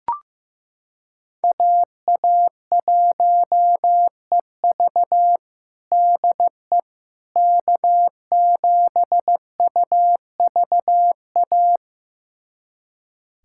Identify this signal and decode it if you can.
Morse code, heard as a keyed tone, about 16 wpm, 700 Hz. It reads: AA1EV DE K7UVA
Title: USA Callsign DE Callsign - 5 Characters - 15wpm